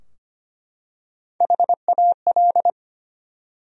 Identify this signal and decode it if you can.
Morse code, heard as a keyed tone, about 25 wpm, 700 Hz. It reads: HAL
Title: USA Common Names - Max 5 Letters - 25wpm